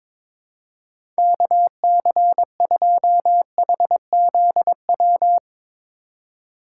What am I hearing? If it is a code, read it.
Morse code, heard as a keyed tone, about 22 wpm, 700 Hz. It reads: KC2HZW
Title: USA Call Signs - 6 Characters - 3x Character Spacing - 22wpm